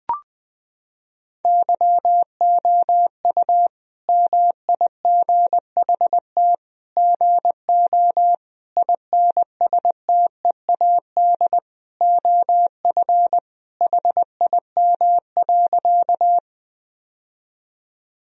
Morse code, heard as a keyed tone, about 20 wpm, 700 Hz. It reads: YOU MIGHT GO INSTEAD OF HIM.
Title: Sentences from Top 1000 Words - Review Part 4 - 20wpm